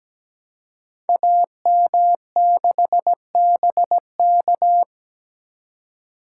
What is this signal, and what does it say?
Morse code, heard as a keyed tone, about 17 wpm, 700 Hz. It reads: AM6BK
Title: Code Groups - 5 Letter:Numbers - 17wpm